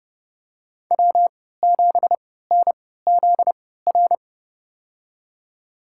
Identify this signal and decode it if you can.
Morse code, heard as a keyed tone, about 30 wpm, spaced as for 17 wpm, 700 Hz. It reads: W7NZR
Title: USA Call Signs - 5 Characters - 3x Character Spacing - 30wpm